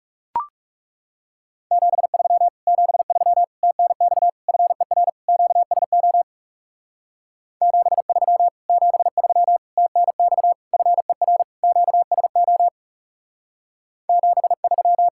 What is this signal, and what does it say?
Morse code, heard as a keyed tone, about 38 wpm, 700 Hz. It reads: 73 73 TNX FER QSO 73 73 TNX FER QSO 73